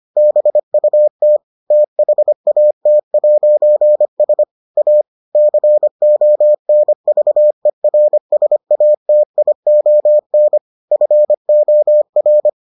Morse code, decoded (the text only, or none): BUT THAT'S A CONVERSATION FOR